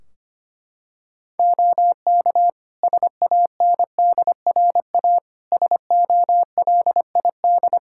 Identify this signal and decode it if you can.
Morse code, heard as a keyed tone, about 25 wpm, 700 Hz. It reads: OK SANDRA SOLID
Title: Practice QSO - Exchange 3 - More Info - 25wpm